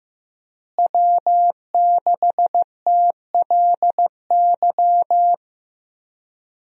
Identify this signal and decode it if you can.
Morse code, heard as a keyed tone, about 15 wpm, 700 Hz. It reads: W6TLY
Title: USA Call Signs - 5 Characters - Repeated 3x - 15wpm